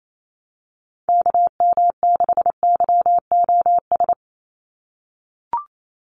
Morse code, heard as a keyed tone, about 28 wpm, 700 Hz. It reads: KM6YOS
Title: USA Call Signs 28wpm